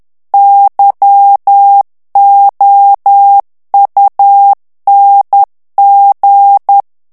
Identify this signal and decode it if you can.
Morse code, heard as a keyed tone, about 10 wpm, 800 Hz. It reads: YOUNG
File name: morse mp3 / ITALIEN texte clair 1 vitesse 10 mots minute